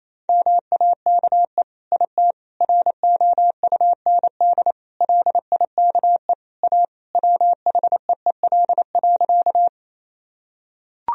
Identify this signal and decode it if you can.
Morse code, heard as a keyed tone, about 28 wpm, 700 Hz. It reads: MAKE IT ROUND LIKE A WHEEL.
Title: Sentences from Top 1000 Words - Review Part 3 - 28wpm